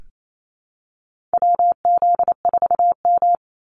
Morse code, heard as a keyed tone, about 28 wpm, 700 Hz. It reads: WZ4M
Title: NA SOTA Activator Callsigns 28wpm